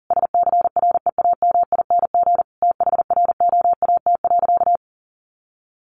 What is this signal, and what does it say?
Morse code, heard as a keyed tone, about 40 wpm, 700 Hz. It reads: SCREAMING THROAT.
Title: A Sound Of Thunder 40wpm